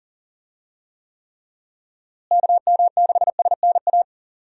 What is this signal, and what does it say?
Morse code, heard as a keyed tone, about 37 wpm, 700 Hz. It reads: KM6SNA
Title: USA Call Signs - 6 Characters - Repeated 3x - 40wpm